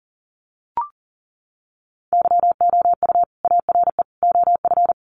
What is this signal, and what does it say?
Morse code, heard as a keyed tone, about 40 wpm, 700 Hz. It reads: YOU ARE OF
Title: Sentences from Top 1400 Words 40wpm